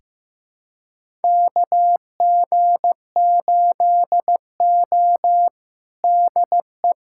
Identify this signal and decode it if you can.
Morse code, heard as a keyed tone, about 16 wpm, 700 Hz. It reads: KG8O DE
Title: USA Callsign DE Callsign - 4 Characters - 15wpm